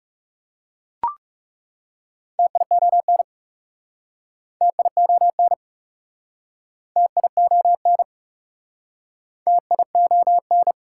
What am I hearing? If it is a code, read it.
Morse code, heard as a keyed tone, about 37 wpm, 700 Hz. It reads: TION TION TION TION
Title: Speed-Racing Prefixes and Suffixes 30wpm